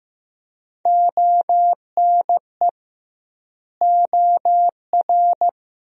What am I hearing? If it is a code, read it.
Morse code, heard as a keyed tone, about 15 wpm, 700 Hz. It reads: ONE OR
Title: 2x Word-Spacing - Sentences from Top 100 Words - 15wpm